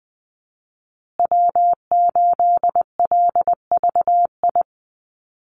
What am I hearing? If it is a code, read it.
Morse code, heard as a keyed tone, about 20 wpm, 700 Hz. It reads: W8LVI
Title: Call Signs Repeated 2x - 20wpm